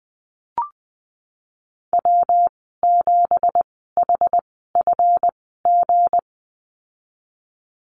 Morse code, heard as a keyed tone, about 20 wpm, spaced as for 14 wpm, 700 Hz. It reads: W7HFG
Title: 179 Lesson 29g - Slash - Call Signs 20wpm